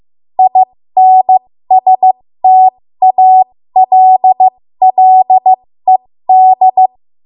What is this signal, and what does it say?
Morse code, heard as a keyed tone, about 15 wpm, spaced as for 13 wpm, 750 Hz. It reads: INSTALLED
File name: Morse / 13WPM